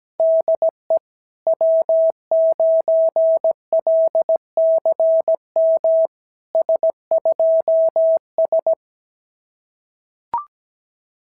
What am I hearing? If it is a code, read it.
Morse code, heard as a keyed tone, about 17 wpm, 690 Hz. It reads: DE W9LCM S2S
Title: SOTA QSO Practice 17wpm